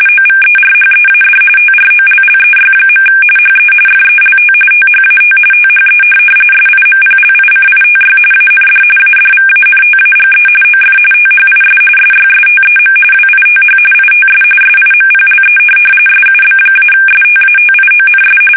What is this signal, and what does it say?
RTTY, about 45.45 baud. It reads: OVERHEAD TROLLEY. HE FELT THAT THE SIMPLICITY AND ADVANTAGE OF THIS SYSTEM WITH ITS SIMPLE SUB-STATIONS
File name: fsk850 / benjamin-g-lamme